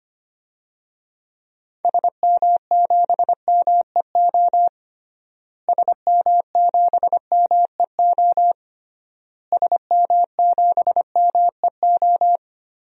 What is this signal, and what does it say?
Morse code, heard as a keyed tone, about 25 wpm, 700 Hz. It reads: SM7MEO SM7MEO SM7MEO
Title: DXCC-100 Call Signs - 6 Characters - Repeated 3x - 25wpm